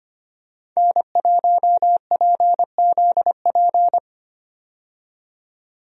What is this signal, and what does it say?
Morse code, heard as a keyed tone, about 25 wpm, 700 Hz. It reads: N1PZP
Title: USA Call Signs - 5 Characters - 25wpm